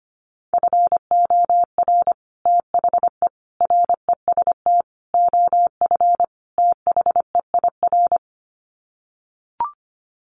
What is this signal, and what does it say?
Morse code, heard as a keyed tone, about 25 wpm, 700 Hz. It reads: FOR THE REST OF THEIR
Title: Sets of 5 Words Encore2 25wpm